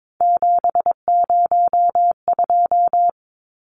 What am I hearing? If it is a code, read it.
Morse code, heard as a keyed tone, about 22 wpm, 700 Hz. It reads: 702
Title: Numbers - 3-Digit 22wpm